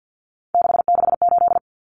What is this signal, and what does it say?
Morse code, heard as a keyed tone, about 48 wpm, 700 Hz. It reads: 668
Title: Common Cut Numbers - 3-Digit 50wpm